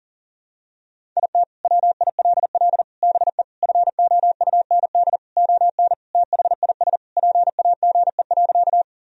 Morse code, heard as a keyed tone, about 40 wpm, 700 Hz. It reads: IT WILL BE FOUND ON THIS PAGE.
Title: Sentences from Top 1000 Words - Review Part 1 - 40wpm